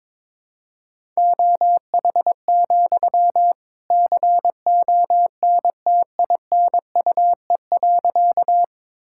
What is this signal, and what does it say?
Morse code, heard as a keyed tone, about 22 wpm, 700 Hz. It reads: OH, CONTINUE.
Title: Sentences from Top 1000 Words - Review Part 6 - 22wpm